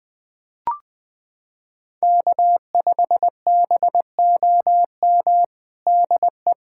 Morse code, heard as a keyed tone, about 20 wpm, 690 Hz. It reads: K5BOM DE
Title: USA Callsign DE Callsign 20wpm